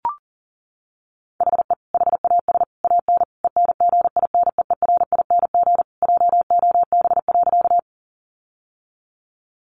Morse code, heard as a keyed tone, about 37 wpm, 700 Hz. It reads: HE HAS AN ENGINEERING JOB.
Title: Sentences from Top 1500 Words 40wpm